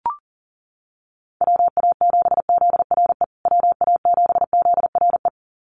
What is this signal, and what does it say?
Morse code, heard as a keyed tone, about 40 wpm, 700 Hz. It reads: WA7ZRE WA7ZRE
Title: USA Call Signs Repeated 2x 40wpm